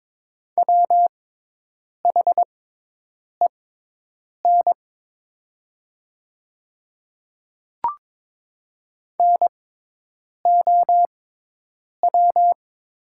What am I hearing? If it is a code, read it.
Morse code, heard as a keyed tone, about 22 wpm, spaced as for 8 wpm, 700 Hz. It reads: WHEN NOW
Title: 6x Character Spacing - Top 100 Words - 22wpm